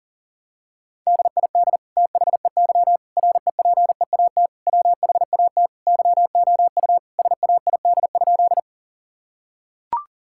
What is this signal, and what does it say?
Morse code, heard as a keyed tone, about 40 wpm, 700 Hz. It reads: DID THEY REPEAT WHAT YOU SAID?
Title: Sentences from Top 1600 Words 40wpm